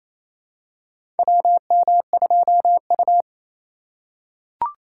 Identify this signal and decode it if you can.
Morse code, heard as a keyed tone, about 28 wpm, 700 Hz. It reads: WM2U
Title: USA Call Signs - 4 Characters - 2x Character Spacing - 28wpm